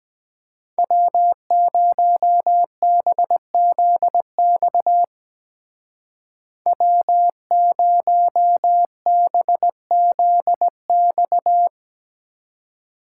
Morse code, heard as a keyed tone, about 20 wpm, 700 Hz. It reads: W0BZX W0BZX
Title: Speed Racing - USA 1x3 Call Signs 17wpm